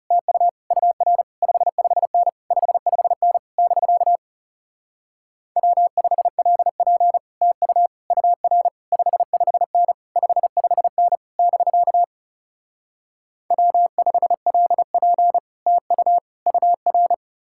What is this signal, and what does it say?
Morse code, heard as a keyed tone, about 40 wpm, 700 Hz. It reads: TU UR 55N 55N <BK> W5LP TU UR 55N 55N <BK> W5LP TU UR